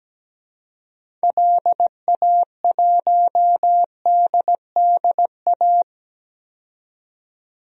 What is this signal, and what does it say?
Morse code, heard as a keyed tone, about 17 wpm, 700 Hz. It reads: LA1DDA
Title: DXCC-100 Call Signs - 6 Characters - 17wpm